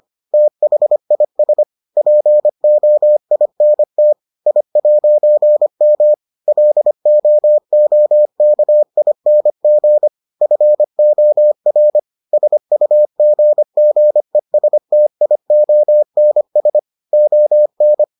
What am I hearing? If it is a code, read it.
Morse code, heard as a keyed tone, about 25 wpm, 600 Hz. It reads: THIS POINT I'M LOOKING FOR SUGGESTIONS ON